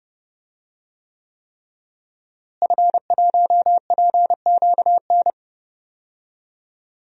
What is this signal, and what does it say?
Morse code, heard as a keyed tone, about 31 wpm, 700 Hz. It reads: F1PQN
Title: DXCC-100 Call Signs - 5 Characters - Repeated 3x - 30wpm